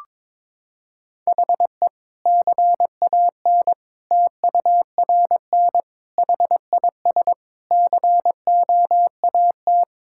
Morse code, heard as a keyed tone, about 22 wpm, 700 Hz. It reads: HE CAN TURN HIS COAT
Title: Sentences from Top 1600 Words 22wpm